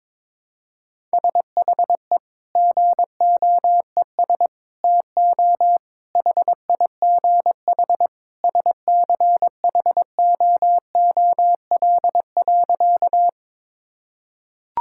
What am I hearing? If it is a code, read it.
Morse code, heard as a keyed tone, about 22 wpm, 700 Hz. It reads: SHE GOES TO HIGH SCHOOL.
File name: practice / 22wpm / Sentences with 2-Word Phrases - 22wpm